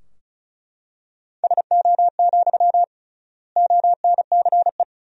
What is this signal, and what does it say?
Morse code, heard as a keyed tone, about 36 wpm, 700 Hz. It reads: SO, ONCE
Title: 3x Word-Spacing - Sentences from Top 300 Words - 35wpm